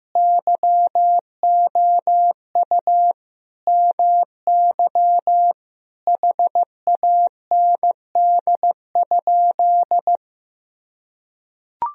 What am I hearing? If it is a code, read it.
Morse code, heard as a keyed tone, about 15 wpm, 700 Hz. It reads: YOU MY HAND?
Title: Sentences from Top 1000 Words - Review Part 4 - 15wpm